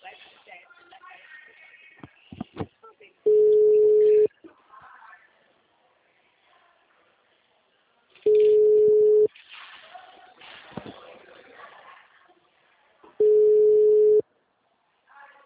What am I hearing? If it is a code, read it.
Ringback tone (Europe, 425 Hz), with speech.